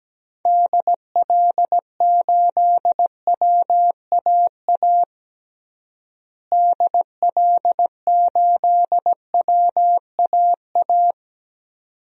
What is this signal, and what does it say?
Morse code, heard as a keyed tone, about 17 wpm, 700 Hz. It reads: DL8WAA DL8WAA
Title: DXCC-100 Call Signs - 6 Characters - Repeated 2x - 17wpm